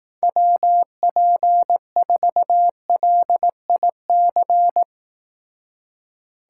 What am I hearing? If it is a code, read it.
Morse code, heard as a keyed tone, about 18 wpm, 700 Hz. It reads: WP4LIC
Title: Speed Racing - USA 2x3 Call Signs 15wpm